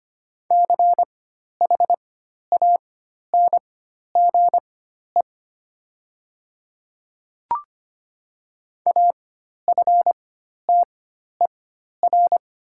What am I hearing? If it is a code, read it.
Morse code, heard as a keyed tone, about 25 wpm, spaced as for 12 wpm, 700 Hz. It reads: CHANGE AFTER
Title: 4x Character Spacing - Top 300 Words - 25wpm